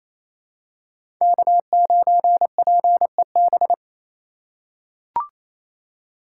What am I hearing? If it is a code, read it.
Morse code, heard as a keyed tone, about 28 wpm, 700 Hz. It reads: K9PEB
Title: USA Call Signs - 5 Characters - 4x Character Spacing - 28wpm